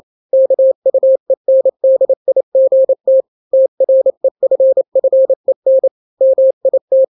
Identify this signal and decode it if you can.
Morse code, heard as a keyed tone, about 28 wpm, 530 Hz. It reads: KUENDIGT TREFFEN MIT
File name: daily / 27wpm